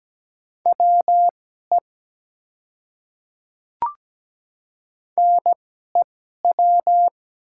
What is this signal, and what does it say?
Morse code, heard as a keyed tone, about 17 wpm, spaced as for 12 wpm, 700 Hz. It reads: WE NEW
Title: 2x Character Spacing - Top 200 Words - 17wpm